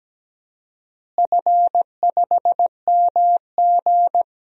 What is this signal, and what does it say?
Morse code, heard as a keyed tone, about 17 wpm, 700 Hz. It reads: F5MG
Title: DXCC-100 Call Signs - 4 Characters - Repeated 2x - 17wpm